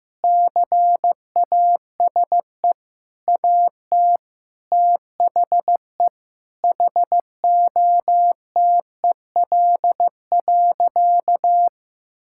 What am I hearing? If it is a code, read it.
Morse code, heard as a keyed tone, about 16 wpm, 700 Hz. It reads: CASE AT THE HOTEL.
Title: Sentences from Top 1600 Words 15wpm